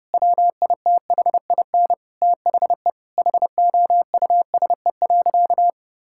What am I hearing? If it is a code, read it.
Morse code, heard as a keyed tone, about 31 wpm, 700 Hz. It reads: WITHIN THE HOUSE.
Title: Sentences from Top 1000 Words - Review Part 5 - 30wpm